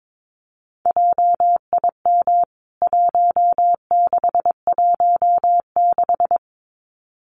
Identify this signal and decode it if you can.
Morse code, heard as a keyed tone, about 22 wpm, 700 Hz. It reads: JIM 1616
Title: CWT Exchange 22wpm